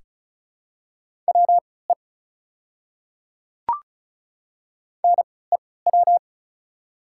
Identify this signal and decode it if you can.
Morse code, heard as a keyed tone, about 36 wpm, spaced as for 20 wpm, 700 Hz. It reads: WE NEW
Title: 3x Character Spacing - Top 200 Words - 35wpm